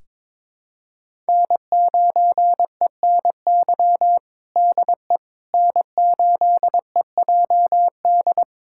Morse code, heard as a keyed tone, about 22 wpm, 700 Hz. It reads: N9ENY DE N8EJD
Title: USA Callsign DE Callsign - 5 Characters - 22wpm